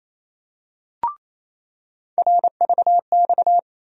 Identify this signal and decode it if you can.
Morse code, heard as a keyed tone, about 28 wpm, 700 Hz. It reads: RVX
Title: USA Call Sign Postfixes - 3 Letters 28wpm